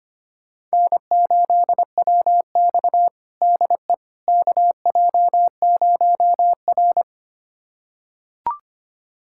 Morse code, heard as a keyed tone, about 25 wpm, 700 Hz. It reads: N8WX DE KJ0R
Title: USA Callsign DE Callsign - 4 Characters - 25wpm